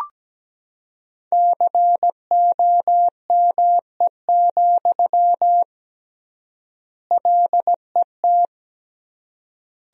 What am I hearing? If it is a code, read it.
Morse code, heard as a keyed tone, about 17 wpm, 700 Hz. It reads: COME, LET
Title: 3x Word-Spacing - Sentences from Top 300 Words - 17wpm